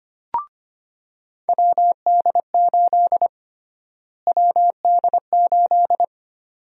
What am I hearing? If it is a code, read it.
Morse code, heard as a keyed tone, about 25 wpm, 700 Hz. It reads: WD8 WD8
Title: USA Call Sign Prefixes - Repeated 2x - 25wpm